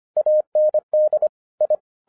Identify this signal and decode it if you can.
Morse code, heard as a keyed tone, about 25 wpm, 610 Hz. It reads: AND I